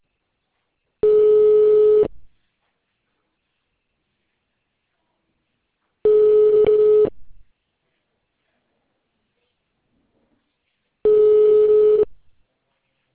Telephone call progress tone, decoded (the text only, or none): ringback tone